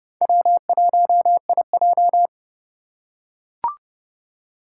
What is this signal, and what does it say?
Morse code, heard as a keyed tone, about 30 wpm, 700 Hz. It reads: W1IJ
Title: USA Call Signs - 4 Characters - 2x Character Spacing - 30wpm